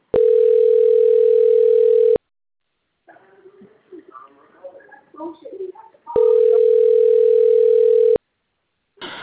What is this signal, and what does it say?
Ringback tone (North America), 440 + 480 Hz, with speech.
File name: out- - - - - .6091